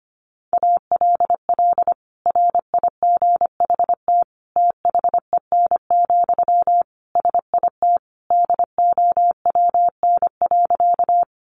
Morse code, heard as a keyed tone, about 25 wpm, 700 Hz. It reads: ALL RIGHT THEN, SIT DOWN.
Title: Sentences from Top 500 Words 25wpm